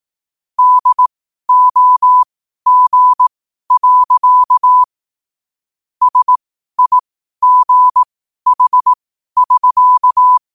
Morse code, heard as a keyed tone, about 18 wpm, spaced as for 12 wpm, 1000 Hz. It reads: DOG. SIGH<SK>
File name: qotdfiles / 12wpm